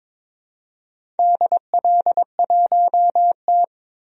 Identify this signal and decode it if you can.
Morse code, heard as a keyed tone, about 21 wpm, 700 Hz. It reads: DL1T
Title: DXCC-100 Call Signs - 4 Characters - 22wpm